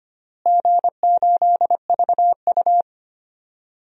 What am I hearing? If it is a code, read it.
Morse code, heard as a keyed tone, about 25 wpm, 700 Hz. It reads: G8VU
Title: DXCC-100 Call Signs - 4 Characters - Repeated 2x - 25wpm